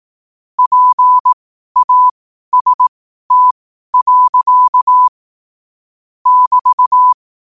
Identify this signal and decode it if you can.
Morse code, heard as a keyed tone, about 18 wpm, spaced as for 12 wpm, 1000 Hz. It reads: PAST. <BT>
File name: qotdfiles / 12wpm